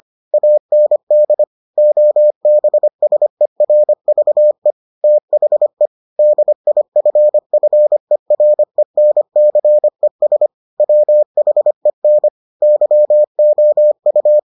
Morse code, heard as a keyed tone, about 25 wpm, 600 Hz. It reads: AND OBSERVE THE DIFFERENCES WHEN YOU